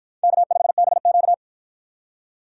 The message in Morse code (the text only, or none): K5BQ